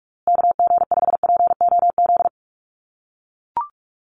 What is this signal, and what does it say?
Morse code, heard as a keyed tone, about 44 wpm, 700 Hz. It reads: KG5POZ
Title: USA Call Signs Repeated 2x 45wpm